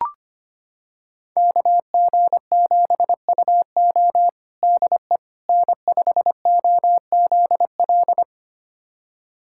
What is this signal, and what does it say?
Morse code, heard as a keyed tone, about 25 wpm, 700 Hz. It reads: KG7UO DE N5OZL
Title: USA Callsign DE Callsign - 5 Characters - 25wpm